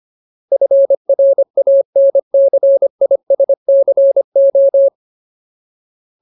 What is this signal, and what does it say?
Morse code, heard as a keyed tone, about 25 wpm, 550 Hz. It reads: FRANCISCO
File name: Ultimate Word List / 25WPM Ultimate